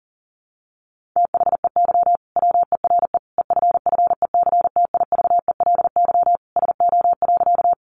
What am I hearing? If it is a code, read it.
Morse code, heard as a keyed tone, about 40 wpm, 700 Hz. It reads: THEY WERE EFFECTIVELY SO.
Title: Sentences from Top 1400 Words 40wpm